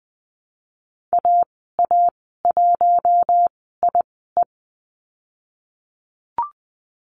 Morse code, heard as a keyed tone, about 20 wpm, spaced as for 14 wpm, 700 Hz. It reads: AA1IE
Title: 048 Lesson 9g - Number 4 - Call Signs 20wpm